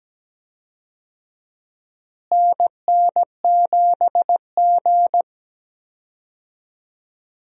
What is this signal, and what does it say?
Morse code, heard as a keyed tone, about 17 wpm, 700 Hz. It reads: NN7G
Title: USA Call Signs - 4 Characters - Repeated 2x - 17wpm